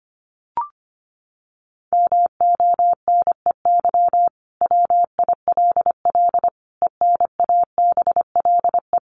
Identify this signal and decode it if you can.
Morse code, heard as a keyed tone, about 25 wpm, 700 Hz. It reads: MONEY WILL ENABLE
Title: Sentences from Top 1500 Words 25wpm